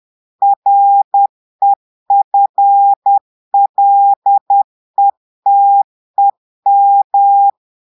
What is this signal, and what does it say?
Morse code, heard as a keyed tone, about 10 wpm, 800 Hz. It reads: REFLETEM
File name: Portugais - texte en clair 1 - vitesse - 10 mots minute